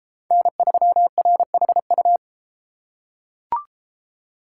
Morse code, heard as a keyed tone, about 32 wpm, 700 Hz. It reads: N3RHU